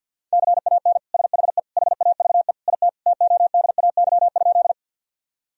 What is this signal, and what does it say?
Morse code, heard as a keyed tone, about 50 wpm, 700 Hz. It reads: CAN SHE HAVE IT TODAY?
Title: Sentences from Top 1000 Words - Review Part 4 - 50wpm